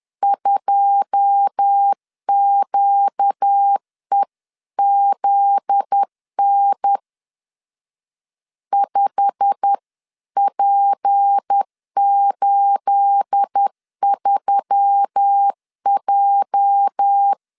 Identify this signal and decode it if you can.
Morse code, heard as a keyed tone, about 10 wpm, 800 Hz. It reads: 2QEZN 5P83J
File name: CW / DET / 1 9 1500C S A 10wpm 4511